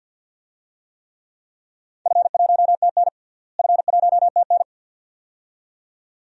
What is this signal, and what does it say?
Morse code, heard as a keyed tone, about 45 wpm, 700 Hz. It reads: F1TN F1TN
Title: DXCC-100 Call Signs - 4 Characters - Repeated 2x - 50wpm